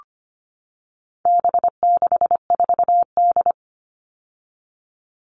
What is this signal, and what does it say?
Morse code, heard as a keyed tone, about 25 wpm, 700 Hz. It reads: B64D
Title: All Cut Numbers - 4-Digit 25wpm